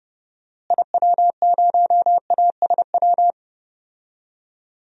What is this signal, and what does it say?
Morse code, heard as a keyed tone, about 30 wpm, 700 Hz. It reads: IW0ASW
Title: DXCC-100 Call Signs - 6 Characters - Repeated 2x - 30wpm